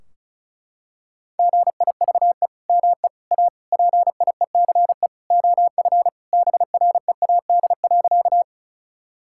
Morse code, heard as a keyed tone, about 34 wpm, 700 Hz. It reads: GIVE ME A PIECE OF BREAD.
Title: Sentences from Top 1600 Words 35wpm